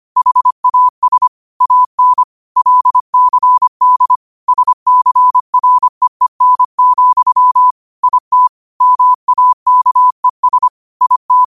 Morse code, heard as a keyed tone, about 25 wpm, 1000 Hz. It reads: HAS AN LCD SCREEN, IT MAKES IT